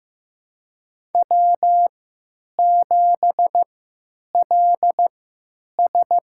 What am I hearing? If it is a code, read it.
Morse code, heard as a keyed tone, about 15 wpm, spaced as for 9 wpm, 700 Hz. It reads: W7LS
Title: USA Call Signs - 4 Characters - 3x Character Spacing - 15wpm